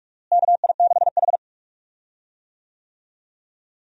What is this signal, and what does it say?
Morse code, heard as a keyed tone, about 46 wpm, 700 Hz. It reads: KI6H
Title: USA Call Signs - 4 Characters - Repeated 3x - 45wpm